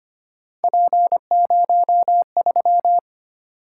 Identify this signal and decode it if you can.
Morse code, heard as a keyed tone, about 25 wpm, 700 Hz. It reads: P03
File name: Three Letter-Number Combinations - 25wpm